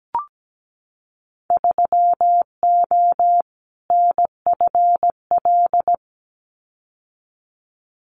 Morse code, heard as a keyed tone, about 17 wpm, 700 Hz. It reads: 3O NFL
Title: Winter Field Day Exchanges 17wpm